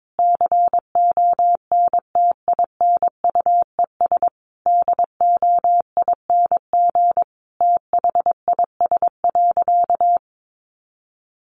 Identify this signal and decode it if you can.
Morse code, heard as a keyed tone, about 22 wpm, 700 Hz. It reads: CONTINUES DOING THIS.
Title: Easy Phrases 22wpm